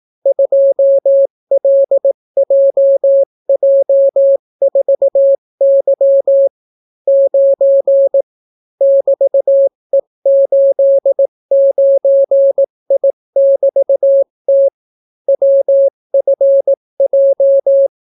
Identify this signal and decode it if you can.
Morse code, heard as a keyed tone, about 18 wpm, 550 Hz. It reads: 2LJJ4Y 9 <BT>E89I<BT>T WFJ